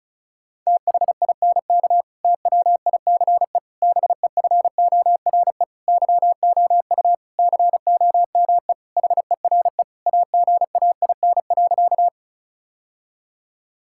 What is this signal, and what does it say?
Morse code, heard as a keyed tone, about 34 wpm, 700 Hz. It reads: THINK TWICE BEFORE YOU COME HERE AGAIN.
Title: Sentences from Top 1600 Words 35wpm